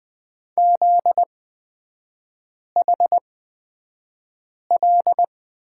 Morse code, heard as a keyed tone, about 20 wpm, spaced as for 5 wpm, 700 Hz. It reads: ZHL
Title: 243 Lesson 39e - Letter z - Three Characters 20wpm